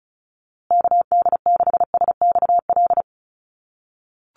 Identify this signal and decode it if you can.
Morse code, heard as a keyed tone, about 34 wpm, 700 Hz. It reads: KD6SXL
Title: USA Call Signs Repeated 2x 35wpm